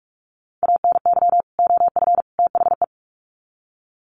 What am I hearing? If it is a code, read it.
Morse code, heard as a keyed tone, about 44 wpm, 700 Hz. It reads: ANY OF THE
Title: Sets of 3 Words 45wpm